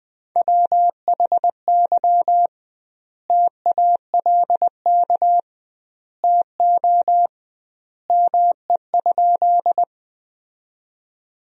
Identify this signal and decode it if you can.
Morse code, heard as a keyed tone, about 20 wpm, 700 Hz. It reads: WHY TALK TO ME?
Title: 2x Word-Spacing - Sentences from Top 400 Words - 20wpm